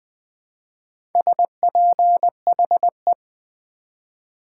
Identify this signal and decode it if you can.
Morse code, heard as a keyed tone, about 20 wpm, 700 Hz. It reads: SPHE
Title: Build A Word - 2 Letters At A Time - 6-Letter Words - 20wpm